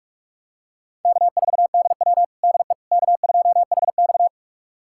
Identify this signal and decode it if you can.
Morse code, heard as a keyed tone, about 42 wpm, 700 Hz. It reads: K4DW DE K2HX
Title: USA Callsign DE Callsign - 4 Characters - 45wpm